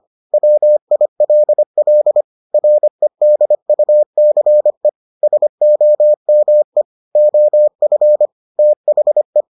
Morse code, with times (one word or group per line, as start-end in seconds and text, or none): 0.3-2.2 WILL
2.5-4.9 REDUCE
5.2-6.8 SOME
7.2-8.3 OF
8.6-9.4 THE